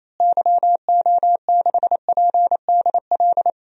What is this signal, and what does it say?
Morse code, heard as a keyed tone, about 28 wpm, 700 Hz. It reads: YO6PDL
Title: DXCC-100 Call Signs - 6 Characters - Repeated 2x - 28wpm